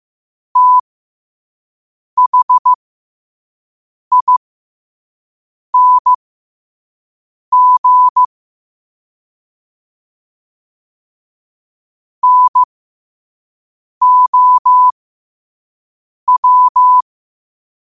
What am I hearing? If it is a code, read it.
Morse code, heard as a keyed tone, about 15 wpm, spaced as for 5 wpm, 1000 Hz. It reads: THING NOW
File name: qotdfiles / 5wpm